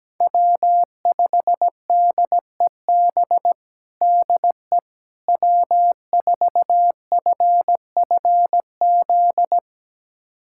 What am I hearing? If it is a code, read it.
Morse code, heard as a keyed tone, about 17 wpm, 700 Hz. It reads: W5DEB DE W4FFZ
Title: USA Callsign DE Callsign - 5 Characters - 17wpm